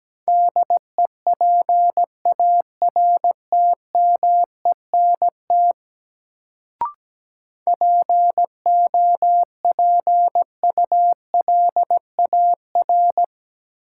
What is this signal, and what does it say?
Morse code, heard as a keyed tone, about 17 wpm, 700 Hz. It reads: DEPARTMENT POPULAR
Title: Sentences from Top 1200 Words - New Word Drill Alt - 17wpm